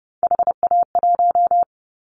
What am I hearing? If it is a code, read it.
Morse code, heard as a keyed tone, about 30 wpm, 700 Hz. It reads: HA1
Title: Callsign Prefixes for Top 25 DXCC - Repeated 3x - 30wpm